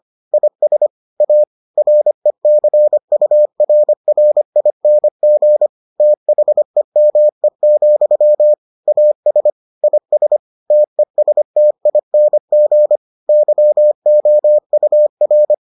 Morse code, heard as a keyed tone, about 25 wpm, 600 Hz. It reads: IS A RECURRING THEME, AS IS TESTING YOUR